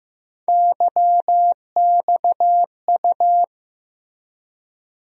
Morse code, heard as a keyed tone, about 15 wpm, 700 Hz. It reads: YXU
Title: USA Call Sign Postfixes - 3 Letters 15wpm